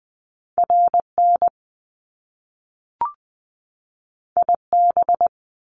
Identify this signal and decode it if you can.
Morse code, heard as a keyed tone, about 20 wpm, 700 Hz. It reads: RN IB
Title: Top Two Letter Combinations in Words 20wpm